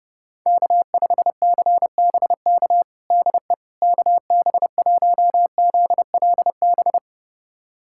Morse code, heard as a keyed tone, about 30 wpm, 700 Hz. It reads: K5CBK DE KB1ZLB
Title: USA Callsign DE Callsign 30wpm